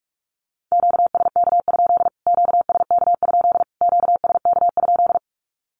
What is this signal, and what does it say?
Morse code, heard as a keyed tone, about 44 wpm, 700 Hz. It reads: QSK? QSK? QSK?
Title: Q-Codes Repeated 3x 45wpm